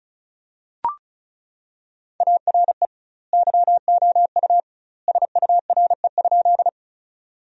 Morse code, heard as a keyed tone, about 36 wpm, 700 Hz. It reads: ARE YOU SURE?
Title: 2x Word-Spacing - Sentences from Top 300 Words - 35wpm